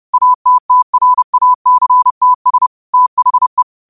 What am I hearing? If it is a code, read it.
Morse code, heard as a keyed tone, about 30 wpm, 1000 Hz. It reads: ATTRACTS THE
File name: qotdfiles / 30wpm